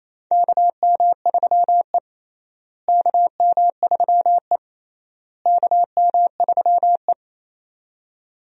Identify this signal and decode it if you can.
Morse code, heard as a keyed tone, about 28 wpm, 700 Hz. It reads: KM3E KM3E KM3E
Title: USA Call Signs - 4 Characters - Repeated 3x - 28wpm